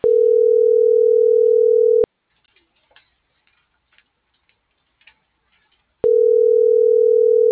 Ringback tone (North America), 440 + 480 Hz.